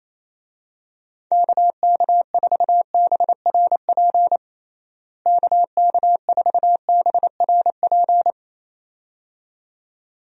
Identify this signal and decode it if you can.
Morse code, heard as a keyed tone, about 28 wpm, 700 Hz. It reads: KK4BRP KK4BRP
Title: USA Call Signs - 6 Characters - Repeated 2x - 28wpm